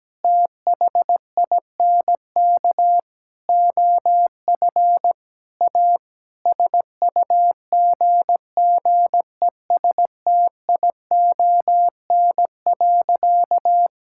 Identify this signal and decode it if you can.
Morse code, heard as a keyed tone, about 17 wpm, 700 Hz. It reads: THINK OF A SUGGESTION.
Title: Sentences from Top 1600 Words 17wpm